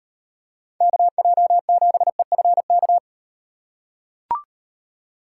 Morse code, heard as a keyed tone, about 35 wpm, 700 Hz. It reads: KJ7EFK